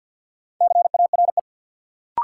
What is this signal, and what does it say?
Morse code, heard as a keyed tone, about 53 wpm, 700 Hz. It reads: CARE
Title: Sentences from Top 300 Words - New Word Drill - 50wpm